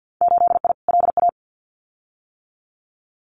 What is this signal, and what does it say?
Morse code, heard as a keyed tone, about 50 wpm, 700 Hz. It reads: 8I LA
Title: Winter Field Day Exchanges Encore 50wpm